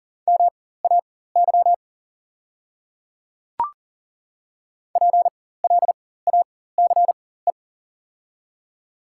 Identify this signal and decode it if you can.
Morse code, heard as a keyed tone, about 41 wpm, spaced as for 19 wpm, 700 Hz. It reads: MAY PLACE
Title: 4x Character Spacing - Top 300 Words - 40wpm